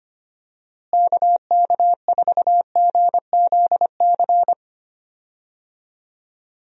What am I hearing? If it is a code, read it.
Morse code, heard as a keyed tone, about 25 wpm, 700 Hz. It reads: KK4GZC
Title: USA Call Signs - 6 Characters - Repeated 2x - 25wpm